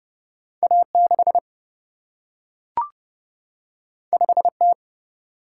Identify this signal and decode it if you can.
Morse code, heard as a keyed tone, about 30 wpm, 700 Hz. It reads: A 6 5 T
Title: Single-Letter Single-Number 30wpm